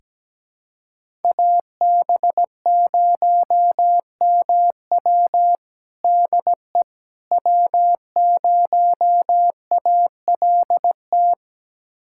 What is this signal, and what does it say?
Morse code, heard as a keyed tone, about 17 wpm, 700 Hz. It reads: AB0MW DE W0ALT
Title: USA Callsign DE Callsign - 5 Characters - 17wpm